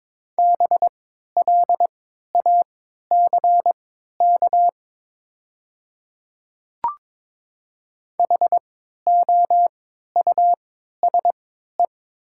Morse code, heard as a keyed tone, about 22 wpm, spaced as for 12 wpm, 690 Hz. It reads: BLACK HOUSE
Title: 3x Character Spacing - Top 500 Words - 22wpm